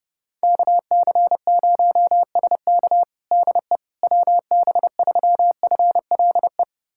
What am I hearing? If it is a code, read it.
Morse code, heard as a keyed tone, about 32 wpm, 700 Hz. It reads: KC0SK DE WB3FLE
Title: USA Callsign DE Callsign 30wpm